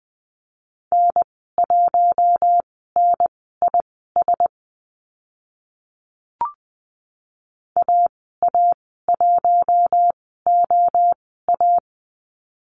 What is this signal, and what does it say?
Morse code, heard as a keyed tone, about 20 wpm, spaced as for 14 wpm, 700 Hz. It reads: N1NIS AA1OA
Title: 082 Lesson 14g - Number 2 - Call Signs 20wpm